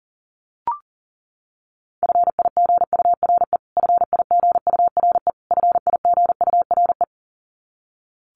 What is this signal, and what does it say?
Morse code, heard as a keyed tone, about 40 wpm, 700 Hz. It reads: FIGURE FIGURE FIGURE
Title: Top 500 Words Repeated 40wpm